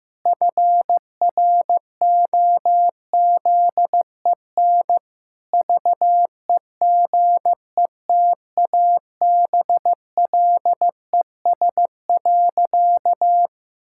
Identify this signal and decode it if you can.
Morse code, heard as a keyed tone, about 14 wpm, 700 Hz. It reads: FROZEN VEGETABLES.
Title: Pangram Challenge v2 - 15wpm